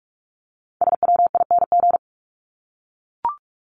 Morse code, heard as a keyed tone, about 44 wpm, 700 Hz. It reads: SWING
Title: Top 1751-2000 Words 45wpm